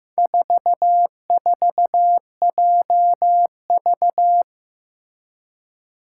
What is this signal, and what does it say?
Morse code, heard as a keyed tone, about 15 wpm, 700 Hz. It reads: 44JV
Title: Four Challenging Letter-Number Combinations 15wpm